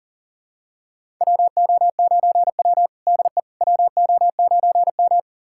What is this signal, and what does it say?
Morse code, heard as a keyed tone, about 40 wpm, 700 Hz. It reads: WO9W DE WO9M
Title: USA Callsign DE Callsign - 4 Characters - 40wpm